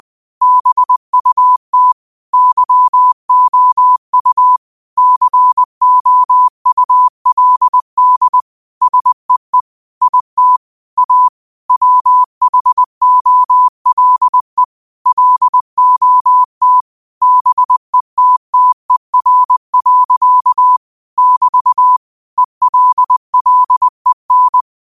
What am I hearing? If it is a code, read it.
Morse code, heard as a keyed tone, about 20 wpm, 1000 Hz. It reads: BUT YOU COULD SEE IT A WHOLE LOT BETTER. <BT> ELLEN
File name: 20wpm